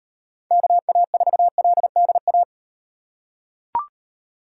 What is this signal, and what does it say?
Morse code, heard as a keyed tone, about 38 wpm, 700 Hz. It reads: KA4LDA